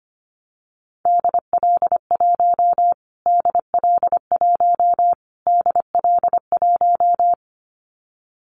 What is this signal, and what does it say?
Morse code, heard as a keyed tone, about 25 wpm, 700 Hz. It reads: DL1 DL1 DL1
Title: Callsign Prefixes for Top 25 DXCC - Repeated 3x - 25wpm